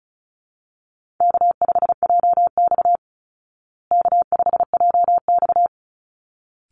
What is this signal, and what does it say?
Morse code, heard as a keyed tone, about 34 wpm, 700 Hz. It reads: K5JX K5JX
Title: Easy Call Signs Repeated 2x - 35wpm